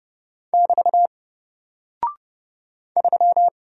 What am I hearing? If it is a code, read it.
Morse code, heard as a keyed tone, about 30 wpm, 700 Hz. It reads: <BT> 3
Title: Single Character (Letter:Number:Punctuation:Prosign) 30wpm